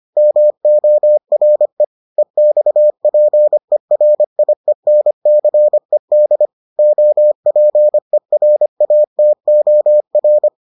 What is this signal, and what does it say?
Morse code, heard as a keyed tone, about 25 wpm, 600 Hz. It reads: MORE EXPERIENCED OPERATOR